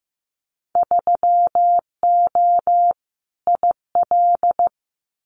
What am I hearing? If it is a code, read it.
Morse code, heard as a keyed tone, about 15 wpm, 700 Hz. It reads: 3O IL
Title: Winter Field Day Exchanges Encore 15wpm